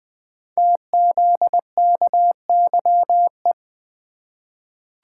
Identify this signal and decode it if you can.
Morse code, heard as a keyed tone, about 20 wpm, 700 Hz. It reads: TZKYE
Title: Code Groups - 5 Letters - 20wpm